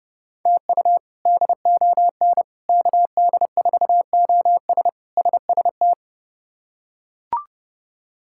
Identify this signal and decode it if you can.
Morse code, heard as a keyed tone, about 31 wpm, 700 Hz. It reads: TU DON KD4OS SST
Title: K1USN SST QSO Practice - 30wpm